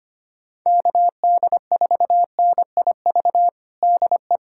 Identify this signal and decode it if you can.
Morse code, heard as a keyed tone, about 25 wpm, 700 Hz. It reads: KD4NIV DE
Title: USA Callsign DE Callsign 25wpm